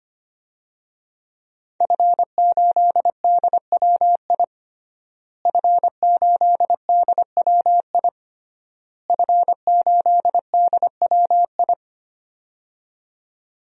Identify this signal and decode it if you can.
Morse code, heard as a keyed tone, about 25 wpm, 700 Hz. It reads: F8DWI F8DWI F8DWI
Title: DXCC-100 Call Signs - 5 Characters - Repeated 3x - 25wpm